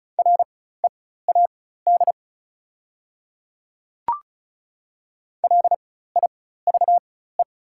Morse code, heard as a keyed tone, about 36 wpm, spaced as for 16 wpm, 700 Hz. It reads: READ LIVE
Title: 4x Character Spacing - Top 300 Words - 35wpm